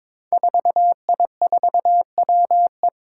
Morse code, heard as a keyed tone, about 23 wpm, 700 Hz. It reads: 4I4WE
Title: Five Challenging Letter-Number Combinations 22wpm